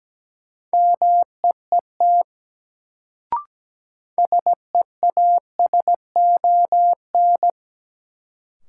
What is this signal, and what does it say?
Morse code, heard as a keyed tone, about 17 wpm, 700 Hz. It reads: MEET SEASON
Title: Sets of 5 Words - New Word Drill - 17wpm